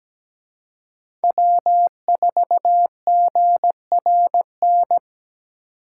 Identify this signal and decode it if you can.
Morse code, heard as a keyed tone, about 20 wpm, 700 Hz. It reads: W4GRN
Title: Speed Racing - USA Call Signs 17wpm